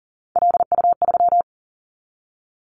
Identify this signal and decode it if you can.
Morse code, heard as a keyed tone, about 37 wpm, 700 Hz. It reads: LU3
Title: Callsign Prefixes for Top 100 DXCC - 40wpm